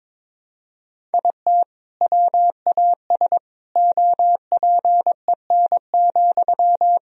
Morse code, heard as a keyed tone, about 22 wpm, 700 Hz. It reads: IT WAS OPEN,
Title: Sentences from Top 1000 Words - Review Part 4 - 22wpm